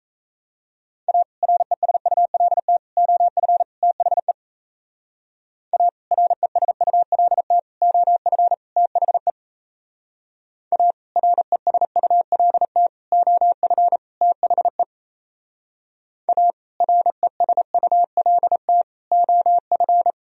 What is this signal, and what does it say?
Morse code, heard as a keyed tone, about 34 wpm, 700 Hz. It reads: A RESULT OF THE A RESULT OF THE A RESULT OF THE A RESULT OF
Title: Speed-Racing - Sets of 4 Words 28wpm